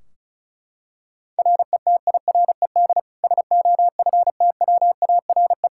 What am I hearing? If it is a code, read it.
Morse code, heard as a keyed tone, about 36 wpm, 700 Hz. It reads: RETIRED SOFTWARE
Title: Practice QSO - Exchange 3 - More Info - 35wpm